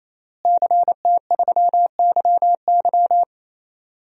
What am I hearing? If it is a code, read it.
Morse code, heard as a keyed tone, about 27 wpm, 700 Hz. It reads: CT3YY
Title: DXCC-100 Call Signs - 5 Characters - Repeated 3x - 28wpm